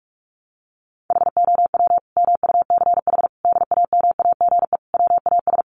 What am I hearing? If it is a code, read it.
Morse code, heard as a keyed tone, about 44 wpm, 700 Hz. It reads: HOW MUCH DAMAGE WAS
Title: Sentences from Top 1100 Words 45wpm